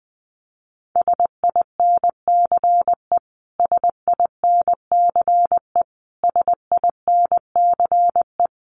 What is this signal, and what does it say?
Morse code, heard as a keyed tone, about 19 wpm, 700 Hz. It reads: SINCE SINCE SINCE
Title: Top 500 Words Repeated 20wpm